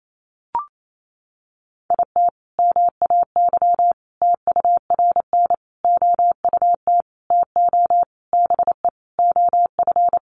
Morse code, heard as a keyed tone, about 28 wpm, 700 Hz. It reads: IT MAY TURN OUT TO BE OF
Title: Sentences from Top 400 Words 28wpm